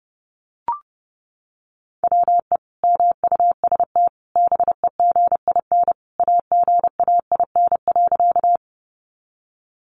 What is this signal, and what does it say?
Morse code, heard as a keyed tone, about 28 wpm, 700 Hz. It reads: WE MUST BEGIN AGAIN.
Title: Sentences from Top 300 Words 30wpm